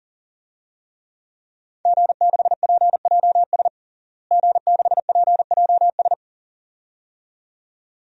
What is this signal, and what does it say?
Morse code, heard as a keyed tone, about 40 wpm, 700 Hz. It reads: G6PJS G6PJS
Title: DXCC-100 Call Signs - 5 Characters - Repeated 2x - 40wpm